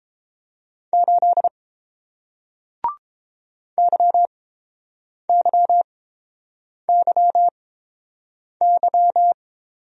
Morse code, heard as a keyed tone, about 30 wpm, spaced as for 7 wpm, 700 Hz. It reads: 8 YYYY